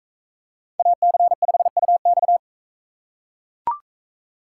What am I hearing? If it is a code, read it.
Morse code, heard as a keyed tone, about 41 wpm, 700 Hz. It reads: AC5UX